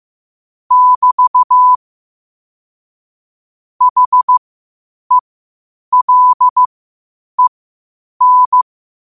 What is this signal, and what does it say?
Morse code, heard as a keyed tone, about 15 wpm, spaced as for 8 wpm, 1000 Hz. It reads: <BT> HELEN
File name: qotdfiles / 8wpm